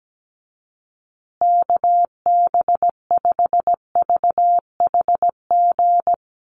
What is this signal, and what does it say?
Morse code, heard as a keyed tone, about 17 wpm, 700 Hz. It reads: KB5VHG
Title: USA Call Signs Repeated 3x 17wpm